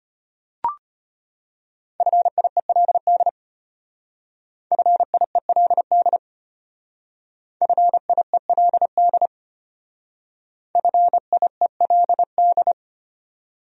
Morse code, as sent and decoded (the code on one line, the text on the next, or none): ..-. .. .-.. -.. / ..-. .. .-.. -.. / ..-. .. .-.. -.. / ..-. .. .-.. -..
FIELD FIELD FIELD FIELD